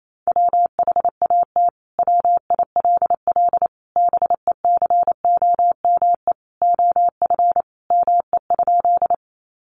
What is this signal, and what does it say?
Morse code, heard as a keyed tone, about 28 wpm, 700 Hz. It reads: WHAT WILL BECOME OF ME?
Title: Sentences from Top 600 Words 28wpm